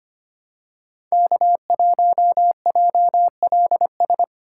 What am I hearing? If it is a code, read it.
Morse code, heard as a keyed tone, about 25 wpm, 700 Hz. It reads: K1JLS
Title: USA Call Signs - 5 Characters - 4x Character Spacing - 25wpm